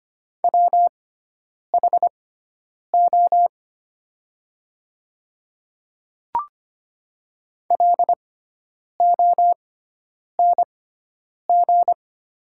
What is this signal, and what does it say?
Morse code, heard as a keyed tone, about 25 wpm, spaced as for 9 wpm, 700 Hz. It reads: WHO LONG
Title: 6x Character Spacing - Top 200 Words - 25wpm